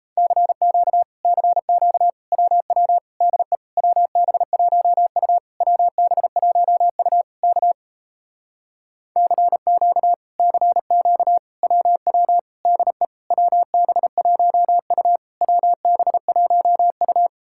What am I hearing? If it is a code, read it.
Morse code, heard as a keyed tone, about 38 wpm, 700 Hz. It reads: CQ CQ WW DE WB1U WB1U K CQ CQ WW DE WB1U WB1U